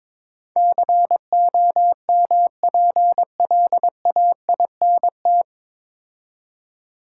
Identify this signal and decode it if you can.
Morse code, heard as a keyed tone, about 23 wpm, 700 Hz. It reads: COMPLAINT
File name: 9 Letter Words - First 3 Letters Given - 22wpm